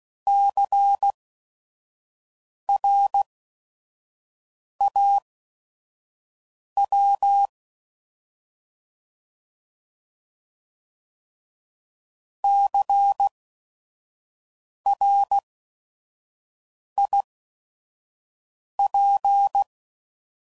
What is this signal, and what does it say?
Morse code, heard as a keyed tone, about 16 wpm, spaced as for 5 wpm, 800 Hz. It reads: CRAW CRIP
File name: Prefixes / By First Letter / Cs